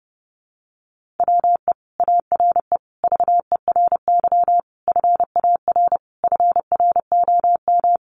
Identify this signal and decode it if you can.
Morse code, heard as a keyed tone, about 30 wpm, 700 Hz. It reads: WE ARE VERY FAR FROM
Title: Sentences from Top 300 Words 30wpm